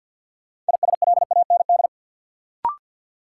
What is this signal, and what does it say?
Morse code, heard as a keyed tone, about 53 wpm, 700 Hz. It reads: ISLAND
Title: Sentences from Top 500 Words - New Word Drill - 50wpm